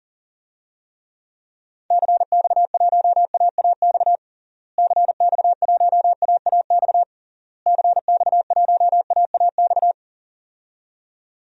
Morse code, heard as a keyed tone, about 40 wpm, 700 Hz. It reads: CX1AAX CX1AAX CX1AAX
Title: DXCC-100 Call Signs - 6 Characters - Repeated 3x - 40wpm